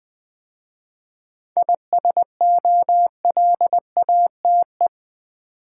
Morse code, heard as a keyed tone, about 20 wpm, 690 Hz. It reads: ISOLATE
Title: 7 Letter Words - First 3 Letters Given - 20wpm